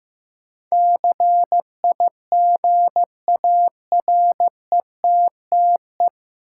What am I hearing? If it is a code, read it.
Morse code, heard as a keyed tone, about 15 wpm, 700 Hz. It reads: CIGARETTE
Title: Sentences from Top 1000 Words - New Word Drill - 15wpm